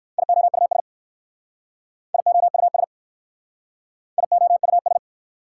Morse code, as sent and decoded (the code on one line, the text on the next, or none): .. --- ..- ... / .. --- ..- ... / .. --- ..- ...
IOUS IOUS IOUS